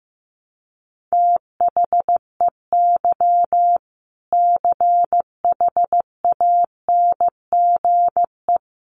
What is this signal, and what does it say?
Morse code, heard as a keyed tone, about 15 wpm, 700 Hz. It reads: THEY CHANGE
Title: Sentences from Top 1200 Words 15wpm